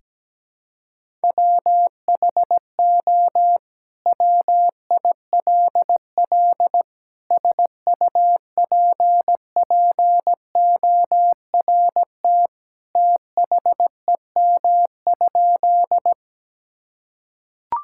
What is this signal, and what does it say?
Morse code, heard as a keyed tone, about 17 wpm, 700 Hz. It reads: WHO WILL SUPPORT THEM?
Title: Sentences from Top 1000 Words - Review Part 5 - 17wpm